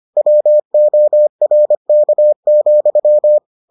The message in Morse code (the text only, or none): WORK,